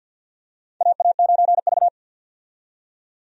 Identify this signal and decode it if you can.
Morse code, heard as a keyed tone, about 53 wpm, 700 Hz. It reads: AA9V
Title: USA Call Signs - 4 Characters - 50wpm